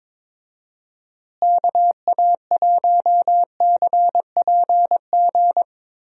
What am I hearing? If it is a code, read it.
Morse code, heard as a keyed tone, about 21 wpm, 700 Hz. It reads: KA1CPG
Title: USA Call Signs - 6 Characters - Repeated 2x - 22wpm